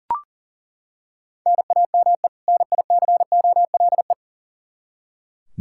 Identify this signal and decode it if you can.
Morse code, heard as a keyed tone, about 37 wpm, 700 Hz. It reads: NAME NICOLE
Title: Practice QSO - Exchange 1 - Beginning - 40wpm